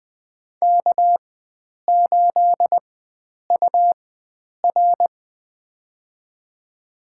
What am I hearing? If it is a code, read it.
Morse code, heard as a keyed tone, about 21 wpm, spaced as for 9 wpm, 700 Hz. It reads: K8UR
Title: USA Call Signs - 4x Character Spacing - 20wpm